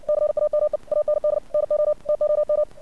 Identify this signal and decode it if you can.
Morse code, heard as a keyed tone, about 56 wpm, 600 Hz. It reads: CAME AND NO TOM
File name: archived / CRSnet